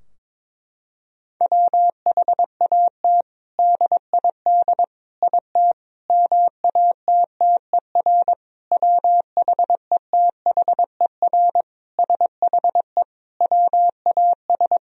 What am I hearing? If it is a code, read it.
Morse code, heard as a keyed tone, about 22 wpm, 700 Hz. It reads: WHAT DID IT MATTER WHETHER SHE WAS
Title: Sentences from Top 1000 Words - Review Part 3 - 22wpm